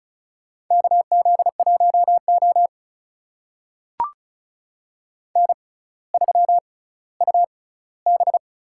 Morse code, heard as a keyed tone, about 36 wpm, spaced as for 18 wpm, 700 Hz. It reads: KZ1O N3UB
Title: USA Call Signs - 4 Characters - 6x Character Spacing - 35wpm